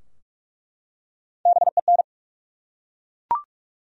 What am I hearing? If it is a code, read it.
Morse code, heard as a keyed tone, about 46 wpm, 700 Hz. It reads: BEN
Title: USA Common Names - Max 5 Letters - 45wpm